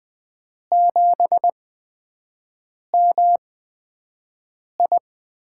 Morse code, heard as a keyed tone, about 20 wpm, spaced as for 5 wpm, 700 Hz. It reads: 7MI
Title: USA Call Signs - 5 Characters - 8x Character Spacing - 20wpm